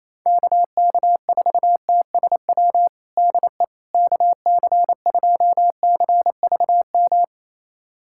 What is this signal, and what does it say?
Morse code, heard as a keyed tone, about 28 wpm, 700 Hz. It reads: KK4TSW DE KC2CVM
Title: USA Callsign DE Callsign - 6 Characters - 28wpm